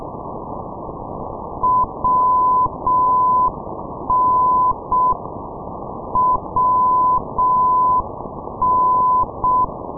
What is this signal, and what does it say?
Morse code, heard as a keyed tone, about 6 wpm, 1010 Hz, with noise. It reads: WN WN
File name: ndb minsk - machulishchi wn-458, sdr kiwi minsk